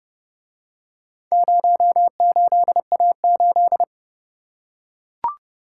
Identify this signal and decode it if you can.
Morse code, heard as a keyed tone, about 29 wpm, 700 Hz. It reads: 08A8